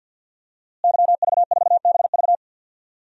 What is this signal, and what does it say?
Morse code, heard as a keyed tone, about 50 wpm, 700 Hz. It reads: YV4BV
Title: DXCC-100 Call Signs - 5 Characters - Repeated 3x - 50wpm